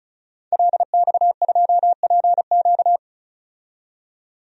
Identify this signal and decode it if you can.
Morse code, heard as a keyed tone, about 34 wpm, 700 Hz. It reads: LX2PQ
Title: DXCC-100 Call Signs - 5 Characters - 35wpm